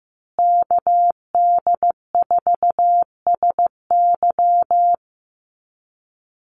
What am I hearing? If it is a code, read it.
Morse code, heard as a keyed tone, about 15 wpm, 700 Hz. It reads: KD4SY
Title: USA Call Signs Repeated 3x 15wpm